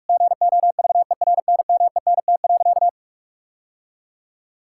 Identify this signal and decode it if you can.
Morse code, heard as a keyed tone, about 42 wpm, 700 Hz. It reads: GOVERNMENT.
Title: Sentences from Top 1600 Words 45wpm